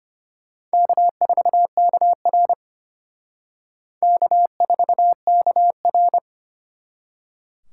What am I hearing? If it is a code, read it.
Morse code, heard as a keyed tone, about 32 wpm, 700 Hz. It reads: K4KR K4KR